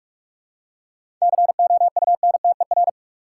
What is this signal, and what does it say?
Morse code, heard as a keyed tone, about 47 wpm, 700 Hz. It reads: COUNTER
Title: Seven Letter Words - 45wpm